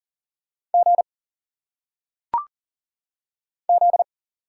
Morse code, heard as a keyed tone, about 39 wpm, 700 Hz. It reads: G Z
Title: Single Letters - Phonetic Pronunciation 40wpm